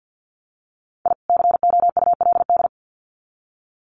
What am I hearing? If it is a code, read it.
Morse code, heard as a keyed tone, about 50 wpm, 700 Hz. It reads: I COULD
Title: Sets of 2 Words 50wpm